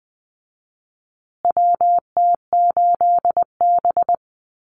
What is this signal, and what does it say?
Morse code, heard as a keyed tone, about 20 wpm, 700 Hz. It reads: WT8B
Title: USA Call Signs Repeated 3x 20wpm